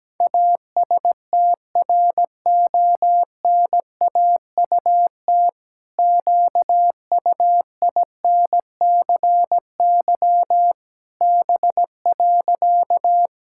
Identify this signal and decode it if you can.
Morse code, heard as a keyed tone, about 17 wpm, 700 Hz. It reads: ASTRONAUT QUINCY B.
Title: Pangram Challenge v2 - 17wpm